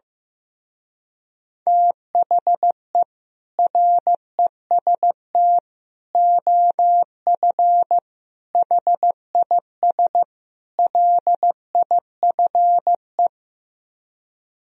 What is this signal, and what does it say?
Morse code, heard as a keyed tone, about 15 wpm, 700 Hz. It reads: THE REST OF HIS LIFE
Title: Speed-Racing - Sets of 5 Words 15wpm